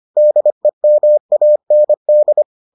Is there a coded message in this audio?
Yes: Morse code from 0.2 to 2.4 s, heard as a keyed tone, about 25 wpm, 600 Hz. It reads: DEMAND